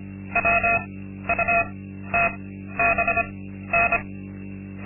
Morse code, heard as a keyed tone, about 25 wpm, spaced as for 13 wpm, 690 Hz, with noise. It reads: WUTBN
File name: Linksmakalnis-2